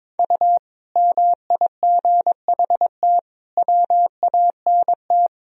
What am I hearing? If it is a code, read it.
Morse code, heard as a keyed tone, about 22 wpm, 700 Hz. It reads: U MIGHT WANT
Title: Sentences from Top 300 Words and Abbreviations 22wpm